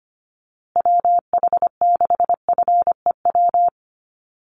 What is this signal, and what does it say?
Morse code, heard as a keyed tone, about 25 wpm, 700 Hz. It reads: WH6FEW
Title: USA Call Signs Repeated 3x 25wpm